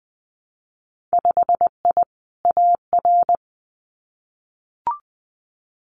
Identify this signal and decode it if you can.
Morse code, heard as a keyed tone, about 20 wpm, 700 Hz. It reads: 5I AR
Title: Winter Field Day Exchanges Encore 20wpm